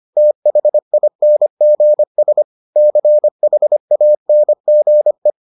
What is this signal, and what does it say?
Morse code, heard as a keyed tone, about 25 wpm, 600 Hz. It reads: THINGS CHANGE